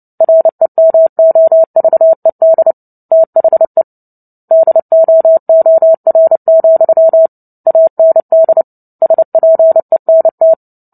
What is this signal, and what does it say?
Morse code, heard as a keyed tone, about 30 wpm, 650 Hz. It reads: REMOVED THE DOOR, AND SPENT
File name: Book Two / 03-Chapter 3 - 29 WPM